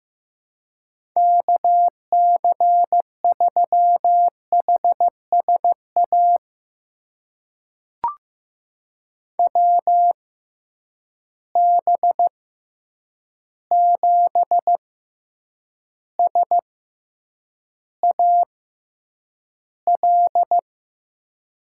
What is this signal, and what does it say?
Morse code, heard as a keyed tone, about 15 wpm, spaced as for 8 wpm, 700 Hz. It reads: KC3HSA WB7SAL
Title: USA Call Signs - 6 Characters - 6x Character Spacing - 15wpm